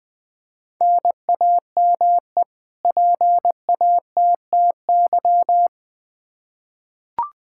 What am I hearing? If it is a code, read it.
Morse code, heard as a keyed tone, about 21 wpm, 700 Hz. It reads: NAME PATTY
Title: K1USN SST QSO Practice 20wpm